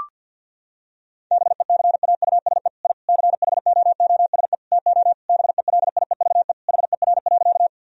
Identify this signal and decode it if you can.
Morse code, heard as a keyed tone, about 50 wpm, 700 Hz. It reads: BECAUSE I CHOOSE TO BELIEVE HER.
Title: Sentences from Top 1000 Words - Review Part 4 - 50wpm